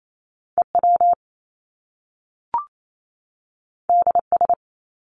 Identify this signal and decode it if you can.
Morse code, heard as a keyed tone, about 28 wpm, 700 Hz. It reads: EW DS
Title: Top Two Letter Combinations in Words 28wpm